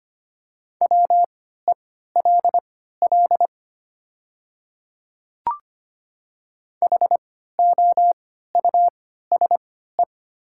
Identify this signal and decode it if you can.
Morse code, heard as a keyed tone, about 25 wpm, spaced as for 14 wpm, 700 Hz. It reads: WELL HOUSE
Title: 3x Character Spacing - Top 300 Words - 25wpm